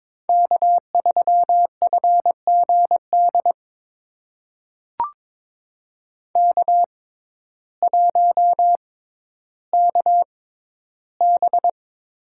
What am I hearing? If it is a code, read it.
Morse code, heard as a keyed tone, about 22 wpm, spaced as for 12 wpm, 700 Hz. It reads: K3FGD K1KB
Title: USA Call Signs - 6x Character Spacing - 22wpm